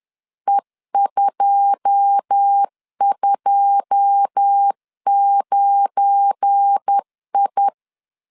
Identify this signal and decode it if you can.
Morse code, heard as a keyed tone, about 10 wpm, 800 Hz. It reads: E229I
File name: LEZIONI / 9 2 10wpm 2736